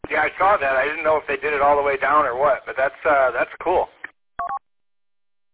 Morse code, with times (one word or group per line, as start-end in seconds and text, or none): none